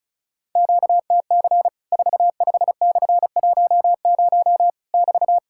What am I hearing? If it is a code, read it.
Morse code, heard as a keyed tone, about 36 wpm, 700 Hz. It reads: QTC 45/10 <BT>
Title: Worked All Europe DX Contest - QTC Exchange - 35wpm